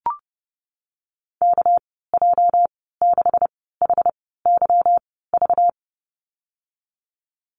Morse code, heard as a keyed tone, about 30 wpm, spaced as for 17 wpm, 700 Hz. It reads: KJ6HYV
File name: Lesson 37g - Letter x - Call Signs 30wpm